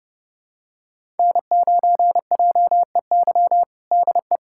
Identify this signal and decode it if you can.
Morse code, heard as a keyed tone, about 30 wpm, 700 Hz. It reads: N9JEY DE
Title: USA Callsign DE Callsign - 5 Characters - 30wpm